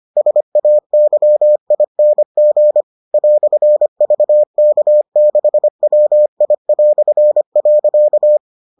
Morse code, heard as a keyed tone, about 25 wpm, 600 Hz. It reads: SAYING "VK6WI".